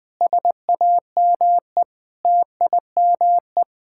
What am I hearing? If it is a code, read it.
Morse code, heard as a keyed tone, about 20 wpm, 700 Hz. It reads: SAME TIME
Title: Sentences with 5-Word Phrases - Spoken Beginning - 20wpm